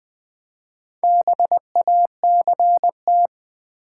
Morse code, heard as a keyed tone, about 20 wpm, 700 Hz. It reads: BACT
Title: Build A Word - 4 Letters At A Time - 8-Letter Words - 20wpm